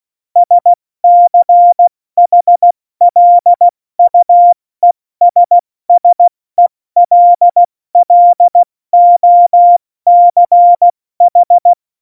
Morse code, heard as a keyed tone, about 16 wpm, 700 Hz. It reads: SCHLUESSELLOCH
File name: Morsekurs / Witze / Witz 02. April 2019, 70 F80